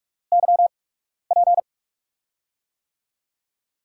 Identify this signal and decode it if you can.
Morse code, heard as a keyed tone, about 42 wpm, spaced as for 12 wpm, 700 Hz. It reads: YP
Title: USA Call Signs - 4 Characters - 8x Character Spacing - 45wpm